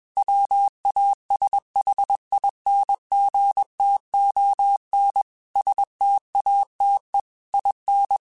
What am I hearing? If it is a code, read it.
Morse code, heard as a keyed tone, about 21 wpm, 800 Hz. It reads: WASHINGTON STATE IN
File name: QSOs / QSOs-mp3s / WD0BGP 1@20